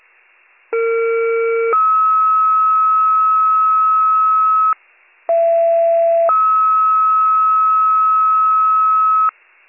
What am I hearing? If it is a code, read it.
Two-tone page, heard as a one-second tone, then a three-second tone, about 460 then 1290 Hz, about 670 then 1290 Hz. It is heard from 0.7 to 9.3 s.